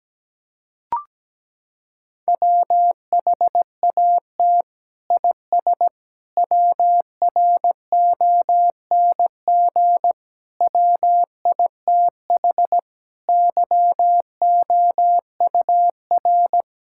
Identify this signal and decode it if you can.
Morse code, heard as a keyed tone, about 17 wpm, 700 Hz. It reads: WHAT IS WRONG WITH YOUR
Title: Sentences from Top 1000 Words - Review Part 5 - 17wpm